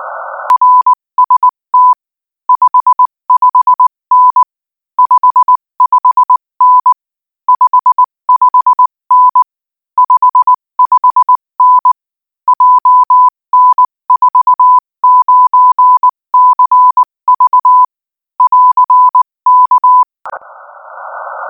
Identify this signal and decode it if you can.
Morse code, heard as a keyed tone, about 19 wpm, 1000 Hz. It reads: RST 55N 55N 55N 55N JN49CV <AR>K
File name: IZ4BEH-13-cm-AS